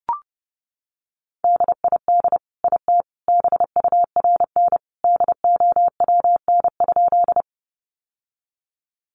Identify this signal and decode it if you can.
Morse code, heard as a keyed tone, about 28 wpm, 700 Hz. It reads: DID IT BURN DOWN?
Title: Sentences from Top 1300 Words 30wpm